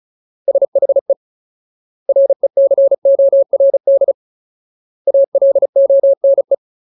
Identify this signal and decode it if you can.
Morse code, heard as a keyed tone, about 34 wpm, 550 Hz. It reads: SHE RECORD ALONE
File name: Ultimate Word List / 35WPM Ultimate